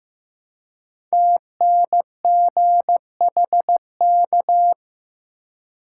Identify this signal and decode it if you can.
Morse code, heard as a keyed tone, about 15 wpm, 700 Hz. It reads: TNGHK
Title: Code Groups - 5 Letter:Numbers - 15wpm